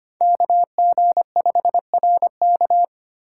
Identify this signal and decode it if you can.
Morse code, heard as a keyed tone, about 25 wpm, 700 Hz. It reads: KG5RK
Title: USA Call Signs - 5 Characters - 8x Character Spacing - 25wpm